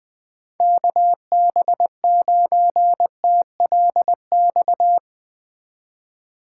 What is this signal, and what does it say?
Morse code, heard as a keyed tone, about 20 wpm, 700 Hz. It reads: KB9TLX